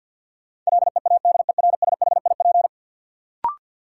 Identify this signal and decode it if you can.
Morse code, heard as a keyed tone, about 46 wpm, 700 Hz. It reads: LEADERSHIP
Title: Sentences from Top 1600 Words - New Word Drill - 50wpm